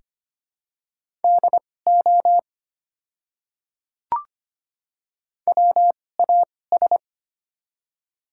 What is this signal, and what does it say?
Morse code, heard as a keyed tone, about 25 wpm, spaced as for 18 wpm, 700 Hz. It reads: DO WAS
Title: 2x Character Spacing - Top 100 Words - 25wpm